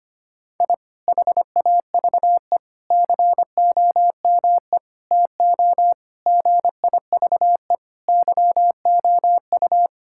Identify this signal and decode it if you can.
Morse code, heard as a keyed tone, about 25 wpm, 700 Hz. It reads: I HAVE COME TO GIVE YOU
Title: Sentences from Top 1000 Words - Review Part 6 - 25wpm